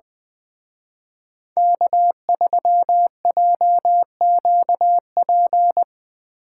Morse code, heard as a keyed tone, about 23 wpm, 700 Hz. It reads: K3JQP
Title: Speed Racing - USA 1x3 Call Signs 20wpm